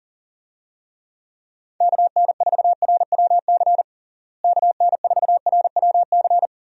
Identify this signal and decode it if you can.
Morse code, heard as a keyed tone, about 40 wpm, 700 Hz. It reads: KN4RWC KN4RWC
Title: USA Call Signs - 6 Characters - Repeated 2x - 40wpm